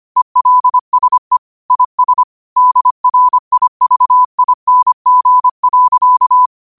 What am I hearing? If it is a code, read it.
Morse code, heard as a keyed tone, about 25 wpm, 1000 Hz. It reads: ELSE IS DRIVING.
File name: 25wpm